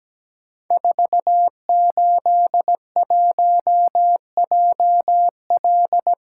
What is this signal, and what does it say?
Morse code, heard as a keyed tone, about 17 wpm, 700 Hz. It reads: 481JL
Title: Code Groups - 5 Letter:Numbers - 17wpm